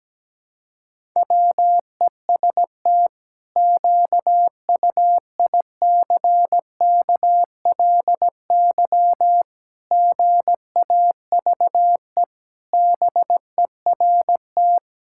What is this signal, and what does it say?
Morse code, heard as a keyed tone, about 17 wpm, 700 Hz. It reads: WEST QUICKLY GAVE BERT
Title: Pangram Challenge v2 - 17wpm